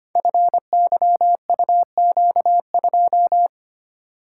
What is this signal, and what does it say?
Morse code, heard as a keyed tone, about 25 wpm, 700 Hz. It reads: FYUQ2
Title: Code Groups - 5 Letter:Numbers - 25wpm